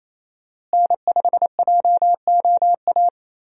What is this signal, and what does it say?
Morse code, heard as a keyed tone, about 28 wpm, 700 Hz. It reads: N5JOA
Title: USA Call Signs - 5 Characters - Repeated 3x - 28wpm